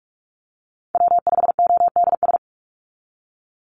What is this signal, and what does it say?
Morse code, heard as a keyed tone, about 44 wpm, 700 Hz. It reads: W5ODS
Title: NA SOTA Activator Callsigns 45wpm